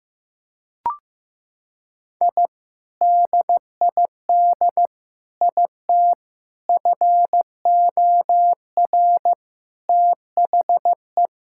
Morse code, heard as a keyed tone, about 16 wpm, 700 Hz. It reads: I DID IT FOR THE
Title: Sentences from Top 1000 Words - Review Part 6 - 15wpm